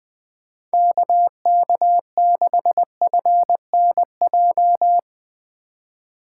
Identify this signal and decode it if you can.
Morse code, heard as a keyed tone, about 22 wpm, 700 Hz. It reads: KK6FNJ
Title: Speed Racing - USA 2x3 Call Signs 20wpm